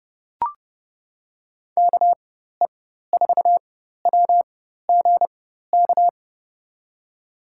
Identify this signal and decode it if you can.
Morse code, heard as a keyed tone, about 30 wpm, spaced as for 14 wpm, 700 Hz. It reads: KE4WGK
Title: USA Call Signs - 6 Characters - 4x Character Spacing - 30wpm